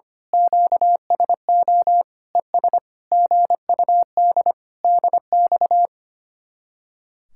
Morse code, heard as a keyed tone, about 24 wpm, 700 Hz. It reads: QSO ES GUD DX
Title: Speed-Racing Common QSO Phrases Encore - 25wpm